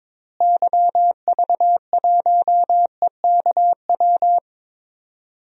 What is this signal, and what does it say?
Morse code, heard as a keyed tone, about 22 wpm, 700 Hz. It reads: YV1EKW
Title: DXCC-100 Call Signs - 6 Characters - Repeated 3x - 22wpm